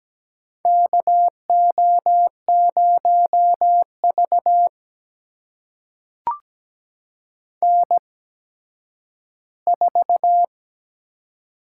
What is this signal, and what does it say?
Morse code, heard as a keyed tone, about 17 wpm, spaced as for 10 wpm, 700 Hz. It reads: KO0V N4
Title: USA Call Signs - 4 Characters - 8x Character Spacing - 17wpm